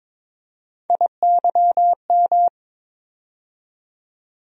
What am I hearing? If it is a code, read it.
Morse code, heard as a keyed tone, about 22 wpm, 700 Hz. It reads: IYM
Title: USA Call Sign Postfixes - 3 Letters 22wpm